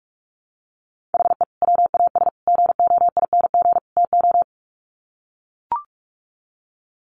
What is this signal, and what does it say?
Morse code, heard as a keyed tone, about 44 wpm, 700 Hz. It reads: HE WAS GOING TO
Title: Sets of 4 Words 45wpm